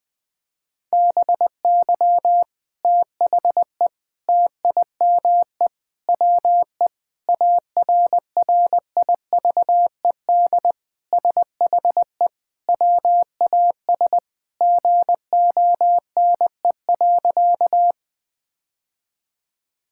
Morse code, heard as a keyed tone, about 20 wpm, 700 Hz. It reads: BY THE TIME WE ARRIVED SHE WAS GONE.
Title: Sentences with 3-Word Phrases - 20wpm